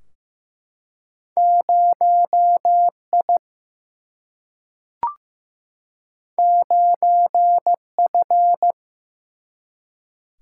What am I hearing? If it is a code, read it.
Morse code, heard as a keyed tone, about 15 wpm, 700 Hz. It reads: 0 I 9 F
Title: Single-Letter Single-Number 15wpm